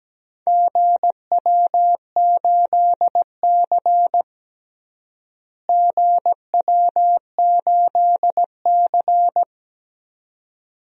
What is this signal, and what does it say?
Morse code, heard as a keyed tone, about 17 wpm, 700 Hz. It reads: GW8C GW8C
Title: DXCC-100 Call Signs - 4 Characters - Repeated 2x - 17wpm